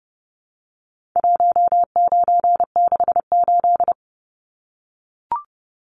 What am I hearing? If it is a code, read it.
Morse code, heard as a keyed tone, about 30 wpm, 700 Hz. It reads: 1968
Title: Numbers - 4-Digit 30wpm